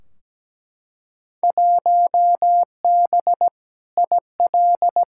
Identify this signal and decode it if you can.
Morse code, heard as a keyed tone, about 17 wpm, 700 Hz. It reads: 1B IL
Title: Field Day Exchanges - 17wpm